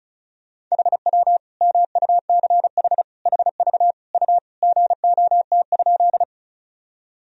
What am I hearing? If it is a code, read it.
Morse code, heard as a keyed tone, about 36 wpm, 700 Hz. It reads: HW MUCH HV U GOT?
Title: Sentences from Top 300 Words and Abbreviations 35wpm